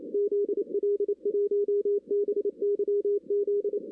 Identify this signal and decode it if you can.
Morse code, heard as a keyed tone, about 30 wpm, 410 Hz, with radio noise. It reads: ZL1BYZ
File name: 2016 arrl dx cw / 2016-ARRL-DX-CW-K5TR- - 0-9